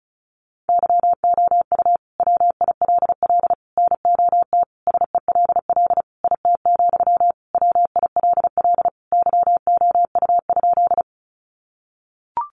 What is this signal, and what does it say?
Morse code, heard as a keyed tone, about 34 wpm, 700 Hz. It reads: YOU WILL NOT SELL IT, WILL YOU?
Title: Sentences from Top 1100 Words 35wpm